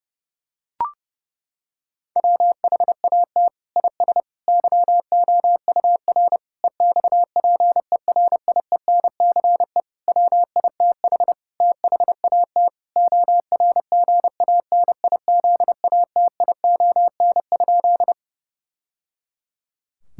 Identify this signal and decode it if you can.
Morse code, heard as a keyed tone, about 30 wpm, 700 Hz. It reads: WHAT IS YOUR EXPERIENCE WITH THAT ORGANIZATION?
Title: Sentences from Top 1000 Words - Review Part 5 - 30wpm